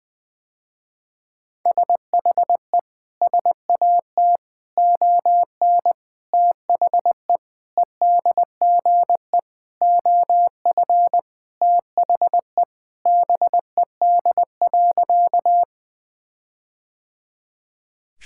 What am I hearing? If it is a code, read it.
Morse code, heard as a keyed tone, about 21 wpm, 700 Hz. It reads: SHE SAT ON THE EDGE OF THE BED.
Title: Sentences with 4-Word Phrases - Spoken Beginning - 20wpm